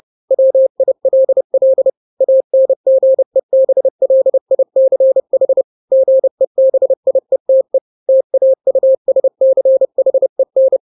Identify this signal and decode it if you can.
Morse code, heard as a keyed tone, about 30 wpm, 530 Hz. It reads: WILL ANGEBLICH GEBIETE TAUSCHEN
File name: 29wpm